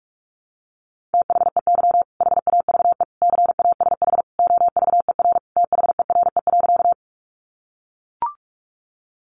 Morse code, heard as a keyed tone, about 44 wpm, 700 Hz. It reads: THEY HAVE CASH OVER THERE.
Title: Sentences from Top 700 Words 45wpm